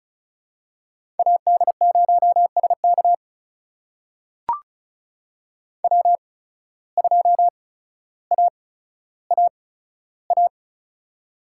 Morse code, heard as a keyed tone, about 36 wpm, spaced as for 15 wpm, 700 Hz. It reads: AD0SK W2AAA
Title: USA Call Signs - 5 Characters - 8x Character Spacing - 35wpm